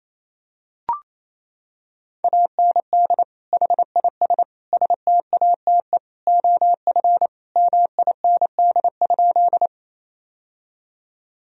Morse code, heard as a keyed tone, about 28 wpm, 700 Hz. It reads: AND HIS STATE OF MIND?
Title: Sentences from Top 1000 Words - Review Part 2 - 28wpm